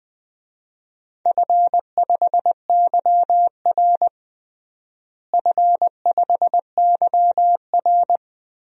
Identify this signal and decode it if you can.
Morse code, heard as a keyed tone, about 20 wpm, 700 Hz. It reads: F5YR F5YR
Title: DXCC-100 Call Signs - 4 Characters - Repeated 2x - 20wpm